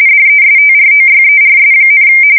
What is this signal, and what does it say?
RTTY, about 45.45 baud. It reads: DESTROYERS,
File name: fsk170 / battleshipnj